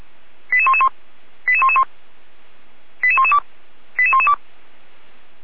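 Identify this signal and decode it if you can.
Five-tone selective call (ZVEI), heard as quick runs of five tones, tones of about 70 ms. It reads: 88111 88111 88112 88112
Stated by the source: ZVEI five-tone call received by SDR.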